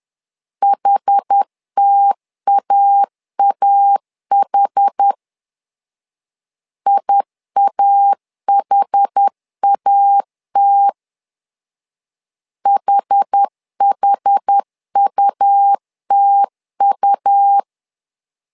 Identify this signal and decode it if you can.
Morse code, heard as a keyed tone, about 10 wpm, 800 Hz. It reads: HTAAH IAHAT HHUTU
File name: LEZIONI / 2 UHDN / 2 UHD 10wpm 1826